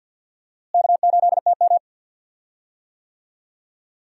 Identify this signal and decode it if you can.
Morse code, heard as a keyed tone, about 51 wpm, 700 Hz. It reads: K8TM
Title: USA Call Signs - 4 Characters - Repeated 2x - 50wpm